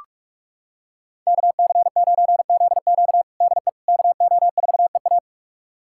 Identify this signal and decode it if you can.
Morse code, heard as a keyed tone, about 46 wpm, 700 Hz. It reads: KC9ZQ DE KO4EA
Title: USA Callsign DE Callsign - 5 Characters - 45wpm